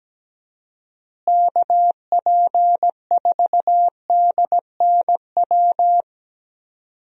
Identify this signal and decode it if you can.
Morse code, heard as a keyed tone, about 17 wpm, 700 Hz. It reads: KP4DNW
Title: USA Call Signs - 6 Characters - Repeated 3x - 17wpm